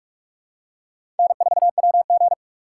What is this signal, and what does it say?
Morse code, heard as a keyed tone, about 46 wpm, 700 Hz. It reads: N4WG
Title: USA Call Signs - 4x Character Spacing - 45wpm